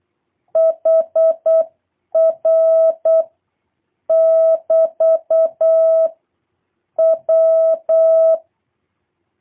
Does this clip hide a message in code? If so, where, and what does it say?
Yes: Morse code from 0.6 to 8.4 s, heard as a keyed tone, about 8 wpm, 650 Hz. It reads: HR <BT> W